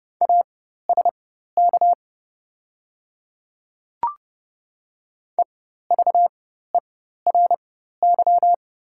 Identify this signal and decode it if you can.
Morse code, heard as a keyed tone, about 30 wpm, spaced as for 14 wpm, 700 Hz. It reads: ASK EVERY
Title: 4x Character Spacing - Top 300 Words - 30wpm